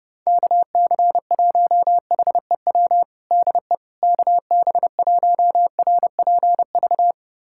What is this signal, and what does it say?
Morse code, heard as a keyed tone, about 32 wpm, 700 Hz. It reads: KC1HEW DE KB1RPV
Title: USA Callsign DE Callsign - 6 Characters - 30wpm